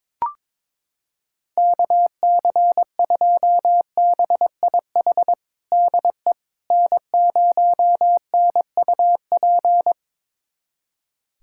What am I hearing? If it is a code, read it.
Morse code, heard as a keyed tone, about 22 wpm, 700 Hz. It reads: KC2BIH DE N0NUP
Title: USA Callsign DE Callsign 22wpm